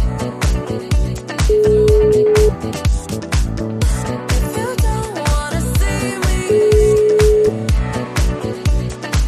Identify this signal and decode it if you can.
Ringback tone (Europe, 425 Hz), with noise.